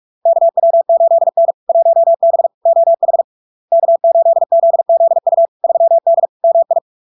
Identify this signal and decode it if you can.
Morse code, heard as a keyed tone, about 46 wpm, 670 Hz. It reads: KW8N 1B OH K8ZZU 3D MI
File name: LARK-FD-2022-Log-in-CW-45wpm-670-Hz